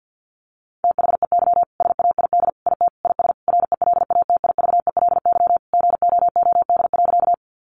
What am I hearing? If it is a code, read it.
Morse code, heard as a keyed tone, about 50 wpm, 700 Hz. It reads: THEY SAID IT IS RELATIVELY GOOD.
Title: Sentences from Top 1100 Words 50wpm